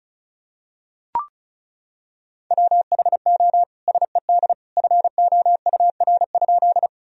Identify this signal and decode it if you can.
Morse code, heard as a keyed tone, about 36 wpm, 700 Hz. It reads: WHO SED FOUR?
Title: Sentences from Top 300 Words and Abbreviations 35wpm